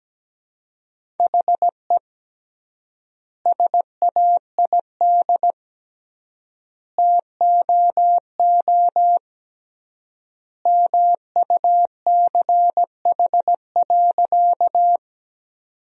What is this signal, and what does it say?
Morse code, heard as a keyed tone, about 17 wpm, 700 Hz. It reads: HE SAID TOO MUCH.
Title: 3x Word-Spacing - Sentences from Top 200 Words - 17wpm